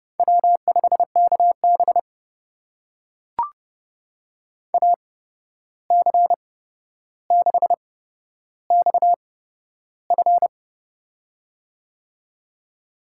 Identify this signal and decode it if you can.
Morse code, heard as a keyed tone, about 29 wpm, spaced as for 12 wpm, 700 Hz. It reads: W5KB AC6XF
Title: USA Call Signs - 8x Character Spacing - 30wpm